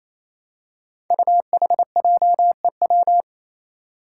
Morse code, heard as a keyed tone, about 28 wpm, 700 Hz. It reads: UHJEW
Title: Five Challenging Letter-Number Combinations 28wpm